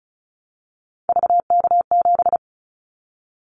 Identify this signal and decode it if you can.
Morse code, heard as a keyed tone, about 36 wpm, 700 Hz. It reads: VK7
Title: Callsign Prefixes for Top 100 DXCC - 35wpm